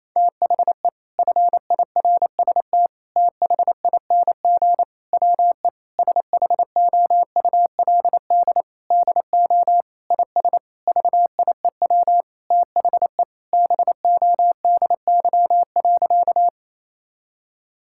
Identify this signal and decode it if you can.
Morse code, heard as a keyed tone, about 28 wpm, 700 Hz. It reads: THE FIRST THING WE SHOULD DO IS VIEW THE BODY.
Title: Sentences from Top 1000 Words - Review Part 6 - 28wpm